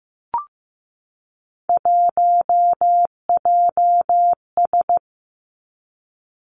Morse code, heard as a keyed tone, about 15 wpm, 700 Hz. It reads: 1JS
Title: Three Challenging Letter-Number Combinations 15wpm